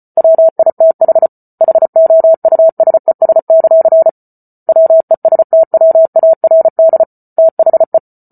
Morse code, heard as a keyed tone, about 34 wpm, 650 Hz. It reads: WITH HOUSES; WESTWARD THE